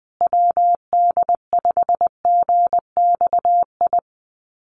Morse code, heard as a keyed tone, about 20 wpm, 700 Hz. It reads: WD5GXI
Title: USA Call Signs Repeated 2x 20wpm